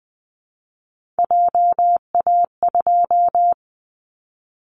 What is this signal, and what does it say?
Morse code, heard as a keyed tone, about 20 wpm, 700 Hz. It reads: JA2
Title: Callsign Prefixes for Top 50 DXCC - 20wpm